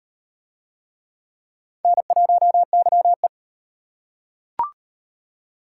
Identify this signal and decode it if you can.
Morse code, heard as a keyed tone, about 38 wpm, 700 Hz. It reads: N1YE